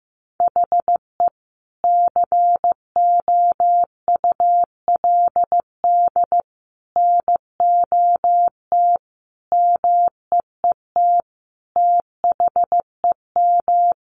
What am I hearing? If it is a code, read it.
Morse code, heard as a keyed tone, about 15 wpm, 700 Hz. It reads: HE COULD NOT MEET THEM
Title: Sentences from Top 1100 Words 15wpm